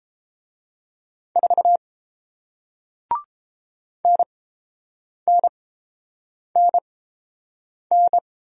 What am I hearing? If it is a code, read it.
Morse code, heard as a keyed tone, about 30 wpm, spaced as for 7 wpm, 700 Hz. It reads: 4 NNNN